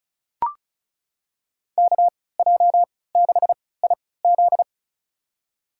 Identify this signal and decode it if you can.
Morse code, heard as a keyed tone, about 36 wpm, spaced as for 20 wpm, 690 Hz. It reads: KJ6IZ
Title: USA Call Signs - 3x Character Spacing - 35wpm